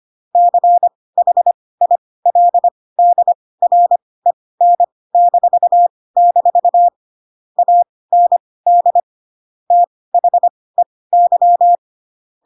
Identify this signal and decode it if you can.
Morse code, heard as a keyed tone, about 25 wpm, spaced as for 18 wpm, 700 Hz. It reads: CHILDREN-- AND THEY
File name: Pinocchio ch3 12 25 ik1hge